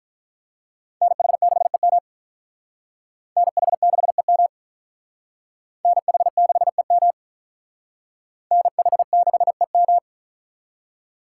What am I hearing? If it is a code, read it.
Morse code, heard as a keyed tone, about 45 wpm, 700 Hz. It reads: NH6EM NH6EM NH6EM NH6EM